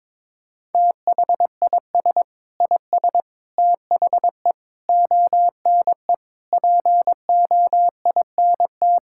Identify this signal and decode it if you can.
Morse code, heard as a keyed tone, about 22 wpm, 700 Hz. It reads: THIS IS THE ONE POINT
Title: Sentences from Top 1000 Words - Review Part 5 - 22wpm